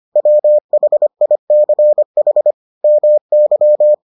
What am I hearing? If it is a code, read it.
Morse code, heard as a keyed tone, about 25 wpm, 600 Hz. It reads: WHICH MY